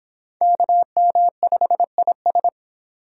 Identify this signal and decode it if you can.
Morse code, heard as a keyed tone, about 25 wpm, 700 Hz. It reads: KM5IS